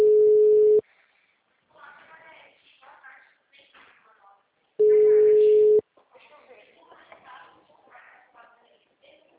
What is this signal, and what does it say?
Ringback tone (Europe, 425 Hz), with speech.